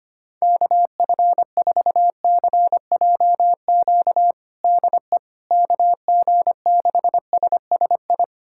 Morse code, heard as a keyed tone, about 25 wpm, 700 Hz. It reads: KF4CJQ DE KG6SSI
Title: USA Callsign DE Callsign - 6 Characters - 25wpm